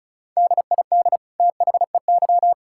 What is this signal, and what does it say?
Morse code, heard as a keyed tone, about 36 wpm, 700 Hz. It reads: DID THEY
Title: Sentences from Top 1000 Words - Review Part 6 - 35wpm